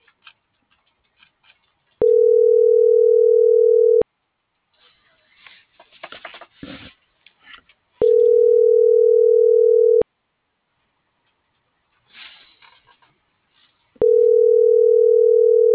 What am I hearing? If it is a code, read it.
Ringback tone (North America), 440 + 480 Hz.